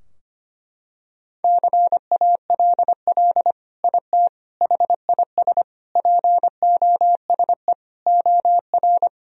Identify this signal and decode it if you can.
Morse code, heard as a keyed tone, about 25 wpm, 700 Hz. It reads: CALL IT HIS POSE OR
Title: Sentences from Top 1000 Words - Review Part 2 - 25wpm